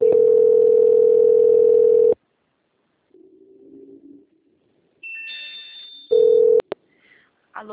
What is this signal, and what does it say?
Ringback tone (North America), 440 + 480 Hz.
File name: out- - - - - .66251